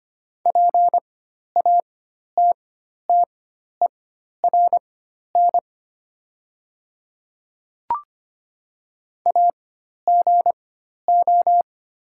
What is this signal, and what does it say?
Morse code, heard as a keyed tone, about 25 wpm, spaced as for 12 wpm, 700 Hz. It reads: PATTERN AGO
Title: 4x Character Spacing - Top 500 Words - 25wpm